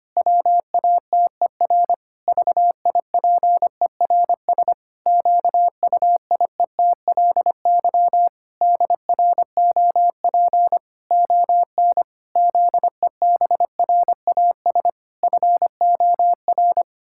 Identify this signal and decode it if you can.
Morse code, heard as a keyed tone, about 25 wpm, 700 Hz. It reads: WATER VIPERS QUIETLY DROP ON ZEBRAS FOR
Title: Pangram Challenge v2 - 25wpm